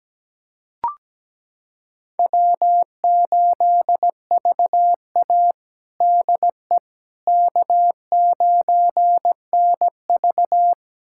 Morse code, heard as a keyed tone, about 17 wpm, 700 Hz. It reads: W8VA DE K9NV
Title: USA Callsign DE Callsign - 4 Characters - 17wpm